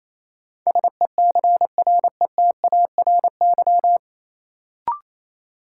Morse code, heard as a keyed tone, about 28 wpm, 700 Hz. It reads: SECRETARY
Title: Sentences from Top 1300 Words - New Word Drill Alt - 28wpm